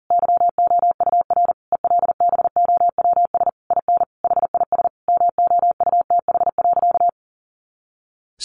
Morse code, heard as a keyed tone, about 40 wpm, 700 Hz. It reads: YOUR ELBOWS IN HIS MOUTH.
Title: A Sound Of Thunder 40wpm